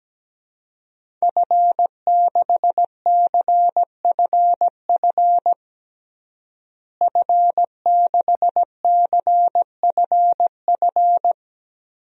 Morse code, heard as a keyed tone, about 17 wpm, 700 Hz. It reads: F6CFF F6CFF
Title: DXCC-100 Call Signs - 5 Characters - Repeated 3x - 17wpm